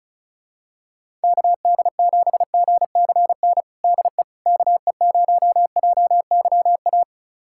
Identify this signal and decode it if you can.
Morse code, heard as a keyed tone, about 36 wpm, 700 Hz. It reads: KD7GCN DE KE0JYA
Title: USA Callsign DE Callsign 35wpm